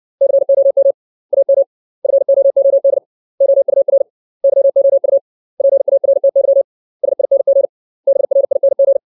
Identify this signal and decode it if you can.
Morse code, heard as a keyed tone, about 60 wpm, 550 Hz. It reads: COM AM FOOD OUR YOU PARTY SING BRING
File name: 632 words / 632 Words 60WPM